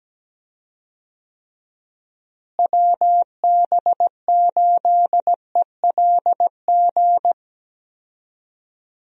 Morse code, heard as a keyed tone, about 17 wpm, 700 Hz. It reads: WB8ELG
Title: USA Call Signs - 6 Characters - Repeated 2x - 17wpm